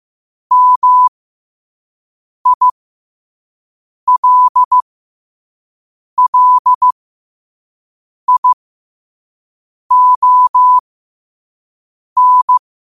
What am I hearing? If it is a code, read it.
Morse code, heard as a keyed tone, about 15 wpm, spaced as for 5 wpm, 1000 Hz. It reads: MILLION
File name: qotdfiles / 5wpm